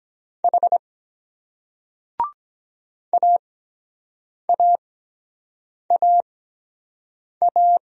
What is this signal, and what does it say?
Morse code, heard as a keyed tone, about 24 wpm, spaced as for 7 wpm, 700 Hz. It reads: H AAAA